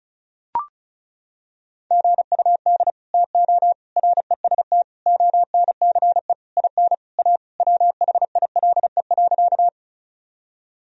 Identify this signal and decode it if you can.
Morse code, heard as a keyed tone, about 36 wpm, 700 Hz. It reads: GUD TO REST ONCE IN A WHILE.
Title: Sentences from Top 500 Words and Abbreviations 35wpm